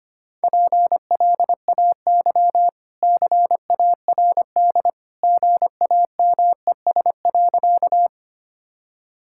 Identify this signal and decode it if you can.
Morse code, heard as a keyed tone, about 25 wpm, 700 Hz. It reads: PLAY CARD GAMES.
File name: Easy Sentences with Hints - 25wpm